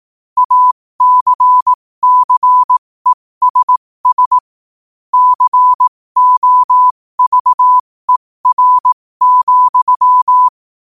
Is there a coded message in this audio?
Yes: Morse code from 0.4 to 10.5 s, heard as a keyed tone, about 18 wpm, spaced as for 15 wpm, 1000 Hz. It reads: ACCESS COVER,